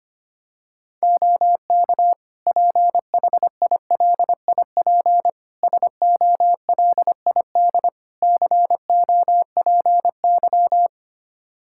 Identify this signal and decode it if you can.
Morse code, heard as a keyed tone, about 25 wpm, 700 Hz. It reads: OK PHILIP SOLID COPY
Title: K1USN SST QSO Practice 25wpm